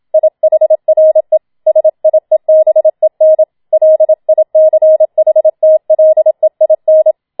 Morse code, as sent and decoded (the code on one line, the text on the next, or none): .. .... .-. / ... .. -... -. / .-.. .. -.-. .... - .-.. .. -.
IHRE SIEBEN LICHTLEIN